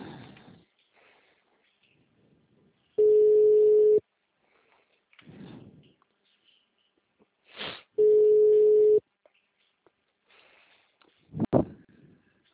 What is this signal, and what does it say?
Ringback tone (Europe, 425 Hz).